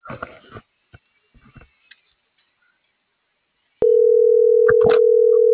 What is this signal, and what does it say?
Ringback tone (North America), 440 + 480 Hz.